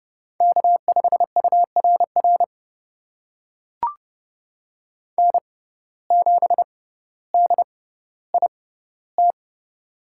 Morse code, heard as a keyed tone, about 30 wpm, spaced as for 15 wpm, 700 Hz. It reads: K5URR N7DIT
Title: USA Call Signs - 5 Characters - 6x Character Spacing - 30wpm